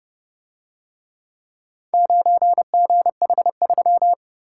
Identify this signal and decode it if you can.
Morse code, heard as a keyed tone, about 29 wpm, 700 Hz. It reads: 9GH3